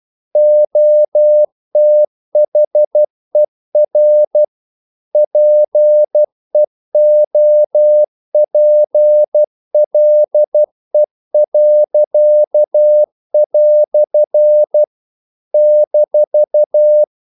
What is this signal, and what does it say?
Morse code, heard as a keyed tone, about 12 wpm, 600 Hz. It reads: OTHER PEOPLE." -
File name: CW / Practice / EnglishQuotations0312